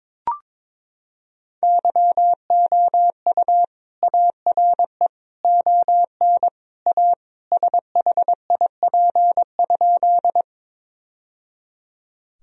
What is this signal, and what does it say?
Morse code, heard as a keyed tone, about 22 wpm, 700 Hz. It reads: YOU ARE ON A SHIP?
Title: Sentences from Top 1000 Words - Review Part 2 - 22wpm